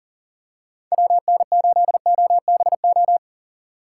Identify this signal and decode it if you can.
Morse code, heard as a keyed tone, about 40 wpm, 700 Hz. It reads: WN8OBO
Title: USA Call Signs - 6 Characters - 8x Character Spacing - 40wpm